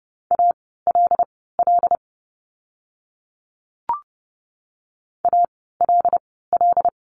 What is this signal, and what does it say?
Morse code, heard as a keyed tone, about 30 wpm, spaced as for 17 wpm, 700 Hz. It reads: ALL ALL
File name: course / 30:10wpm / 237 Lesson 38f - Letter q - Words 30wpm